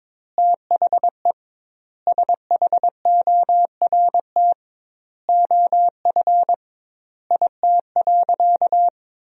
Morse code, heard as a keyed tone, about 22 wpm, 700 Hz. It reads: THE SHORT OF IT.
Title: 2x Word-Spacing - Sentences from Top 400 Words - 22wpm